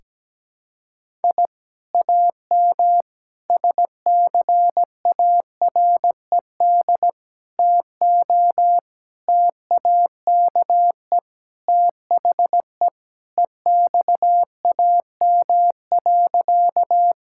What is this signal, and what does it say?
Morse code, heard as a keyed tone, about 17 wpm, 700 Hz. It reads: I AM SCARED TO TAKE THE EXAM.
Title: Sentences from Top 1000 Words - Review Part 5 - 17wpm